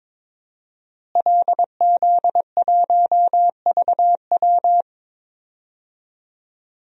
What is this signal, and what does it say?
Morse code, heard as a keyed tone, about 22 wpm, 700 Hz. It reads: LZ1VW
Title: DXCC-100 Call Signs - 5 Characters - Repeated 3x - 22wpm